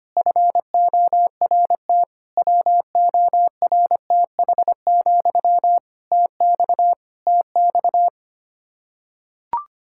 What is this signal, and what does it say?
Morse code, heard as a keyed tone, about 25 wpm, 700 Hz. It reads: FORT WORTH, TX TX
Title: Practice QSO - Exchange 1 - Beginning - 25wpm